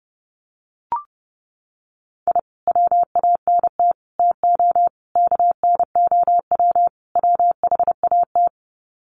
Morse code, heard as a keyed tone, about 30 wpm, 700 Hz. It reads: I WANT TO KNOW WHAT
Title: Sets of 5 Words Encore2 30wpm